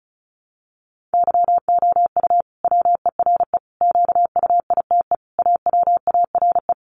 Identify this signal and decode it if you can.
Morse code, heard as a keyed tone, about 34 wpm, 700 Hz. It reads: YOU WERE QUITE AWARE
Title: Sentences from Top 1200 Words 35wpm